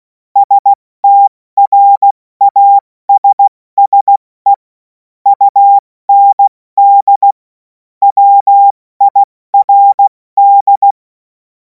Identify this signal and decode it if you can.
Morse code, heard as a keyed tone, about 16 wpm, 800 Hz. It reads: STRASSE UND WIRD
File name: Morsekurs / Witze / Witz 11. August 2026 70 F80